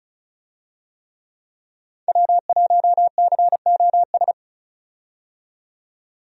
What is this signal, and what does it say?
Morse code, heard as a keyed tone, about 36 wpm, 700 Hz. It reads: W1COS
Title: USA Call Signs - 5 Characters - 35wpm